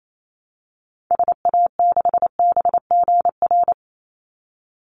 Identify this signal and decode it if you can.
Morse code, heard as a keyed tone, about 28 wpm, 700 Hz. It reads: SA6BGR
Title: Call Signs - 28wpm